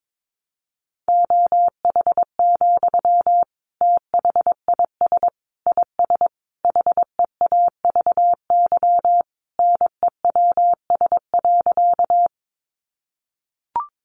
Morse code, heard as a keyed tone, about 22 wpm, 700 Hz. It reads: OH, THIS IS HEAVY NEWS.
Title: Sentences from Top 1100 Words 22wpm